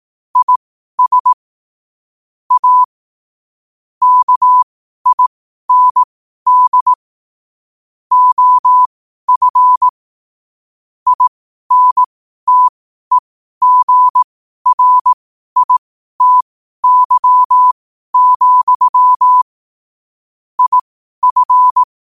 Morse code, heard as a keyed tone, about 18 wpm, spaced as for 12 wpm, 1000 Hz. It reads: IS A KIND OF INTEGRITY, IF
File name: qotdfiles / 12wpm